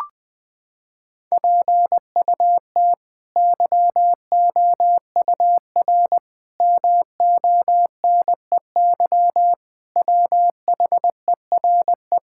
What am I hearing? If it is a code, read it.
Morse code, heard as a keyed tone, about 19 wpm, 700 Hz. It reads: PUT YOUR MONEY WHERE
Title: Sets of 8 Words - 20wpm